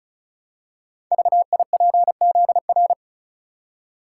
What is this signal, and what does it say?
Morse code, heard as a keyed tone, about 36 wpm, 690 Hz. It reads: VIPZR
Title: Code Groups - 5 Letters - 35wpm